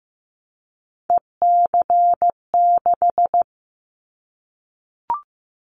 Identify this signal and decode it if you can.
Morse code, heard as a keyed tone, about 15 wpm, 700 Hz. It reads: EC6
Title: Callsign Prefixes for Top 50 DXCC - 15wpm